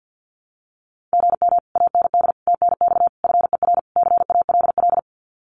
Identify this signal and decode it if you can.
Morse code, heard as a keyed tone, about 48 wpm, 700 Hz. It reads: GM AND TNX FER CALL
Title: Common QSO Phrases Encore 50wpm